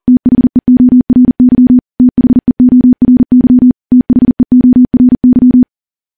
Morse code, heard as a keyed tone, about 39 wpm, 250 Hz. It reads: THEORY THEORY THEORY
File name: morse / morse-training-40WPM